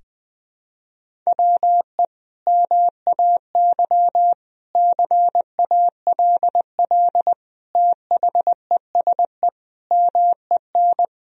Morse code, heard as a keyed tone, about 21 wpm, 700 Hz. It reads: WE MAY CALL THESE MEN
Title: Sentences from Top 1000 Words - Review Part 1 - 20wpm